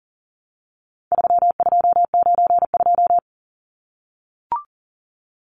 Morse code, heard as a keyed tone, about 40 wpm, 700 Hz. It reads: 3292
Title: Numbers - 4-Digit 40wpm